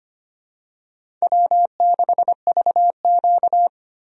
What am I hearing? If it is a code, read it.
Morse code, heard as a keyed tone, about 25 wpm, 700 Hz. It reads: W6VQ
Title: USA Call Signs - 4 Characters - 25wpm